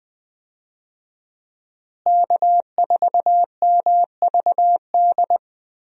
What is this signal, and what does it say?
Morse code, heard as a keyed tone, about 20 wpm, 700 Hz. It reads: K4MVD
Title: USA Call Signs - 5 Characters - Repeated 2x - 20wpm